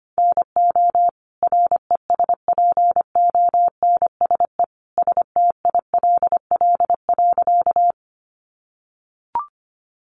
Morse code, heard as a keyed tone, about 25 wpm, 700 Hz. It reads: NO RESPONSE STILL.
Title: Sentences from Top 1400 Words 25wpm